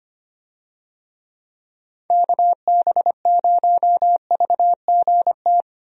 Morse code, heard as a keyed tone, about 25 wpm, 700 Hz. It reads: KB0VGT
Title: USA Call Signs - 6 Characters - 25wpm